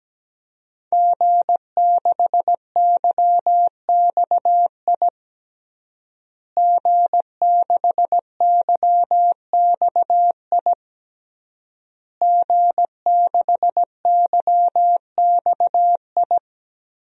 Morse code, heard as a keyed tone, about 17 wpm, 700 Hz. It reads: G6YXI G6YXI G6YXI
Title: DXCC-100 Call Signs - 5 Characters - Repeated 3x - 17wpm